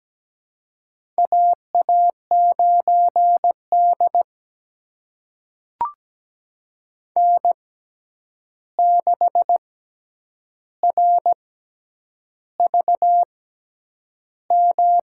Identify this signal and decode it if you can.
Morse code, heard as a keyed tone, about 17 wpm, spaced as for 8 wpm, 700 Hz. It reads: AA9D N6RVM
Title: USA Call Signs - 6x Character Spacing - 17wpm